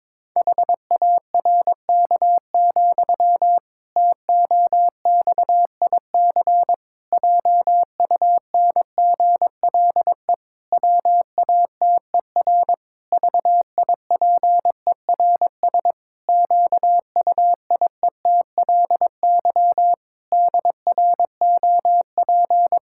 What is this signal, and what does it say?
Morse code, heard as a keyed tone, about 22 wpm, 700 Hz. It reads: HARK, TOXIC JUNGLE WATER VIPERS QUIETLY DROP
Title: Pangram Challenge v2 - 22wpm